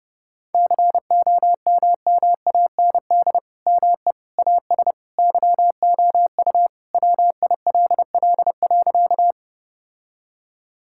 Morse code, heard as a keyed tone, about 32 wpm, 700 Hz. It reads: COMMAND ME AS YOU WILL.
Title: Sentences from Top 1600 Words 30wpm